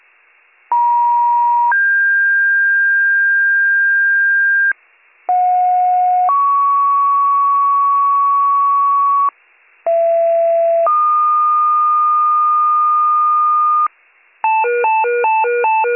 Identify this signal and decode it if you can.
Two-tone page, heard as a one-second tone, then a three-second tone, about 950 then 1600 Hz, about 730 then 1120 Hz, about 670 then 1220 Hz. It is heard from 0.7 to 13.9 s.